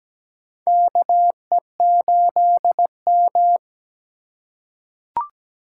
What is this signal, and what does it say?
Morse code, heard as a keyed tone, about 17 wpm, 700 Hz. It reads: KE8M
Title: K1USN SST QSO Practice - 17wpm